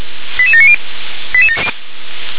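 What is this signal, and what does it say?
Five-tone selective call (ZVEI), heard as quick runs of five tones, tones of about 70 ms. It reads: 88790
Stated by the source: ZVEI five-tone selective call received on SDR.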